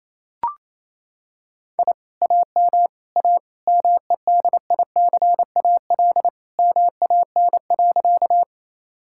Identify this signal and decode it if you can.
Morse code, heard as a keyed tone, about 27 wpm, 700 Hz. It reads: I AM A MEDICAL MAN.
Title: Sentences from Top 1000 Words - Review Part 5 - 28wpm